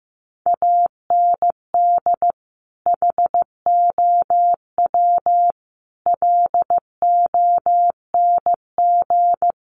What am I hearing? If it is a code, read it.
Morse code, heard as a keyed tone, about 15 wpm, 700 Hz. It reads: AND HOW LONG
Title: Sentences from Top 1200 Words 15wpm